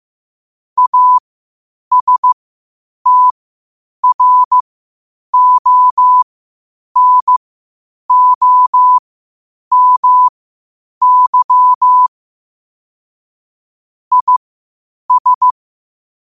Morse code, heard as a keyed tone, about 15 wpm, spaced as for 8 wpm, 1000 Hz. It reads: ASTRONOMY IS
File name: qotdfiles / 8wpm